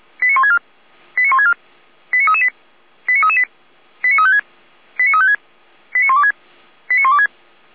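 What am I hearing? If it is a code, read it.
Five-tone selective call (ZVEI), heard as quick runs of five tones, tones of about 70 ms. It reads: 89164 89164 89338 89338 89356 89356 89126 89126
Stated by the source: ZVEI five-tone call received on an SDR, from the Feuerwehr Orlamünde fire brigade.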